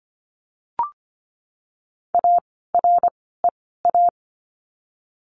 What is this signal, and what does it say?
Morse code, heard as a keyed tone, about 25 wpm, spaced as for 16 wpm, 700 Hz. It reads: AREA
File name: course / 25:10wpm / 094 Lesson 16f - Letter c - Words 25wpm